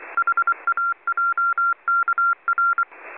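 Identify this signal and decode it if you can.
Morse code, heard as a keyed tone, about 24 wpm, 1380 Hz. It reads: HAJKR
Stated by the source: Morse from numbers stations listening discussion.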